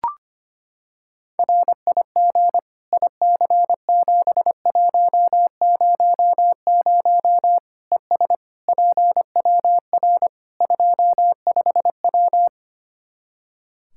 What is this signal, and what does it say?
Morse code, heard as a keyed tone, about 25 wpm, 700 Hz. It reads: RIG IC7100 ES PWR 25W
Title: K1USN SST QSO Practice 25wpm